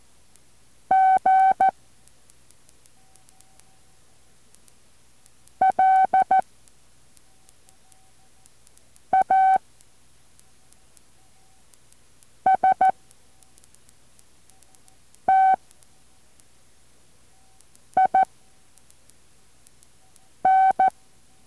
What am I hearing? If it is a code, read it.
Morse code, heard as a keyed tone, about 14 wpm, 760 Hz. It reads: G L A S T I N
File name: intl morse code2